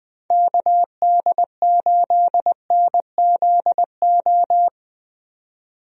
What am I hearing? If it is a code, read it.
Morse code, heard as a keyed tone, about 20 wpm, 700 Hz. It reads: KD8NZO
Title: USA Call Signs - 6 Characters - 20wpm